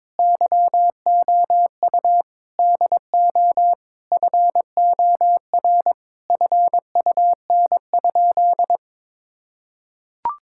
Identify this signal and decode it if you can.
Morse code, heard as a keyed tone, about 22 wpm, 700 Hz. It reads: YOU DO FOR FUN?
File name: practice / 22wpm / Sentences with 4-Word Phrases - 22wpm